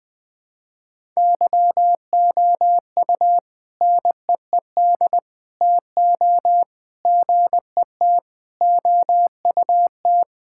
Morse code, heard as a keyed tone, about 20 wpm, 700 Hz. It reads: YOU NEED TO GET OUT
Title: Sentences with 3-Word Phrases - 20wpm